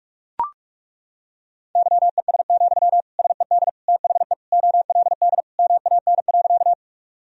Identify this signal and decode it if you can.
Morse code, heard as a keyed tone, about 43 wpm, 700 Hz. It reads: YES, SED THE OLD MAN.
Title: Sentences from Top 500 Words and Abbreviations 45wpm